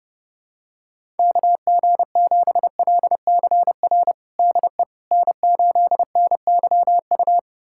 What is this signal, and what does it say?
Morse code, heard as a keyed tone, about 32 wpm, 700 Hz. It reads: KG7LCR DE N8NYU
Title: USA Callsign DE Callsign 30wpm